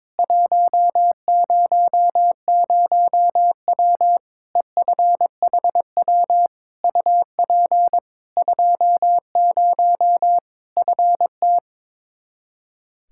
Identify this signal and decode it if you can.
Morse code, heard as a keyed tone, about 21 wpm, 700 Hz. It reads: 100W EFHW UP 20 FT
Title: K1USN SST QSO Practice 22wpm